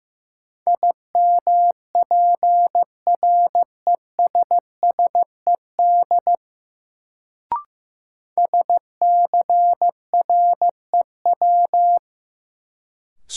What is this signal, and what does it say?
Morse code, heard as a keyed tone, about 15 wpm, 700 Hz. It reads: IMPRESSED SCREW
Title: Sentences from Top 1600 Words - New Word Drill - 15wpm